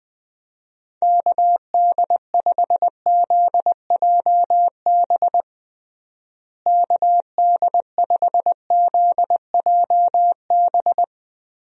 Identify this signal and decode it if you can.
Morse code, heard as a keyed tone, about 20 wpm, 700 Hz. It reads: KD5ZJB KD5ZJB
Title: USA Call Signs - 6 Characters - Repeated 3x - 20wpm